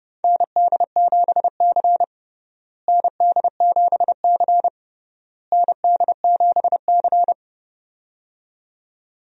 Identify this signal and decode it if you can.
Morse code, heard as a keyed tone, about 30 wpm, 700 Hz. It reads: ND7C ND7C ND7C
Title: USA Call Signs - 4 Characters - Repeated 3x - 30wpm